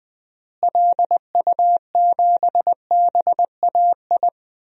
Morse code, heard as a keyed tone, about 21 wpm, 700 Hz. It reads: LU7BAI
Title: DXCC-100 Call Signs - 6 Characters - Repeated 2x - 20wpm